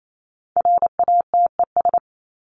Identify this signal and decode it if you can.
Morse code, heard as a keyed tone, about 28 wpm, 700 Hz. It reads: RATES
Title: ES:S Suffix - Top 100 Words with Base Words 28wpm